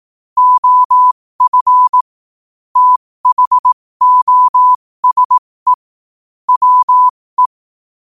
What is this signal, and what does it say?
Morse code, heard as a keyed tone, about 18 wpm, spaced as for 15 wpm, 1000 Hz. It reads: OF THOSE WE
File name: qotdfiles / 15wpm